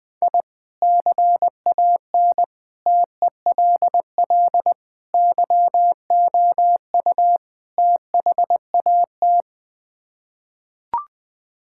Morse code, heard as a keyed tone, about 21 wpm, 700 Hz. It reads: I CAN TELL YOU THAT
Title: Build A Phrase - 5-Word Phrases - 20wpm